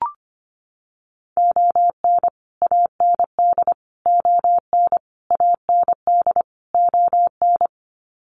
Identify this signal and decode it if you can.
Morse code, heard as a keyed tone, about 25 wpm, 700 Hz. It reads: ON AND ON AND ON
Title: Sets of 5 Words Encore2 25wpm